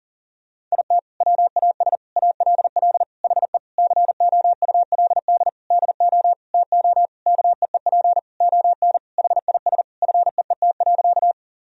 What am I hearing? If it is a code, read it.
Morse code, heard as a keyed tone, about 40 wpm, 700 Hz. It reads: IT WAS ALL HE COULD DO TO KEEP ON HIS FEET.
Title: Sentences from Top 1000 Words - Review Part 1 - 40wpm